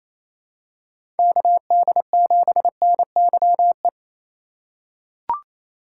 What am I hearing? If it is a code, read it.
Morse code, heard as a keyed tone, about 28 wpm, 700 Hz. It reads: KD7NYE
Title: USA Call Signs - 3x Character Spacing - 28wpm